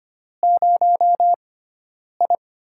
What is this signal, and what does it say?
Morse code, heard as a keyed tone, about 25 wpm, spaced as for 9 wpm, 700 Hz. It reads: 0I
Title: USA Call Signs - 4 Characters - 6x Character Spacing - 25wpm